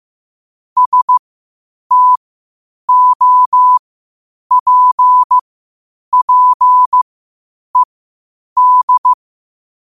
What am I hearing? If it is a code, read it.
Morse code, heard as a keyed tone, about 15 wpm, spaced as for 8 wpm, 1000 Hz. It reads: STOPPED
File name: qotdfiles / 8wpm